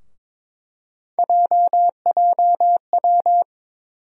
Morse code, heard as a keyed tone, about 21 wpm, 700 Hz. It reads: JJW
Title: USA Call Sign Postfixes - 3 Letters 22wpm